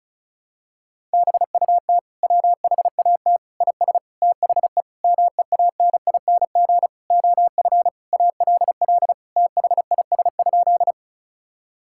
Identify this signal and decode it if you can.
Morse code, heard as a keyed tone, about 36 wpm, 700 Hz. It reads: BUT WHAT IS THE MEANING OF ALL THIS?
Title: Sentences from Top 1000 Words - Review Part 6 - 35wpm